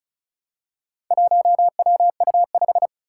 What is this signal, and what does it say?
Morse code, heard as a keyed tone, about 34 wpm, 700 Hz. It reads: 1WU5
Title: Four Challenging Letter-Number Combinations 35wpm